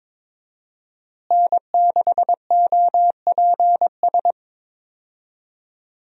Morse code, heard as a keyed tone, about 22 wpm, 700 Hz. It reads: N6OPS
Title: USA Call Signs - 5 Characters - Repeated 2x - 22wpm